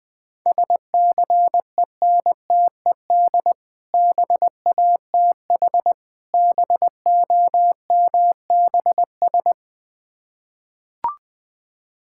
Morse code, heard as a keyed tone, about 20 wpm, 700 Hz. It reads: SCENTED BATH BOMBS
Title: Difficult Words with Hints - 20wpm